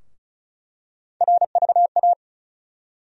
Morse code, heard as a keyed tone, about 36 wpm, 700 Hz. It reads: RVA
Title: USA Call Sign Postfixes - 3 Letters 35wpm